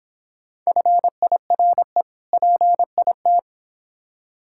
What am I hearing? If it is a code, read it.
Morse code, heard as a keyed tone, about 26 wpm, 700 Hz. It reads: FIRE PIT